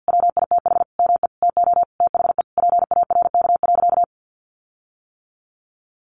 Morse code, heard as a keyed tone, about 50 wpm, 700 Hz. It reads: WITH ME TO THE PARK.
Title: Sentences from Top 1300 Words 50wpm